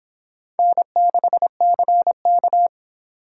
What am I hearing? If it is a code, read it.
Morse code, heard as a keyed tone, about 25 wpm, 700 Hz. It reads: N6CK